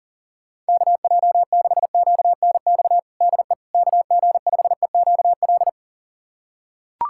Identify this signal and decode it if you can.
Morse code, heard as a keyed tone, about 40 wpm, 700 Hz. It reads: KJ6QNX DE KG5EQL
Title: USA Callsign DE Callsign - 6 Characters - 40wpm